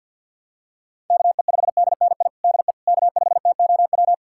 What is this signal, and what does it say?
Morse code, heard as a keyed tone, about 50 wpm, 700 Hz. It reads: KE5DNI DE K5TOW
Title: USA Callsign DE Callsign 50wpm